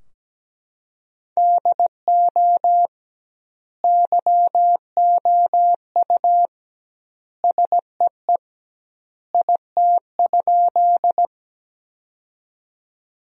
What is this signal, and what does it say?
Morse code, heard as a keyed tone, about 17 wpm, 700 Hz. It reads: DO YOU SEE IT?
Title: 2x Word-Spacing - Sentences from Top 100 Words - 17wpm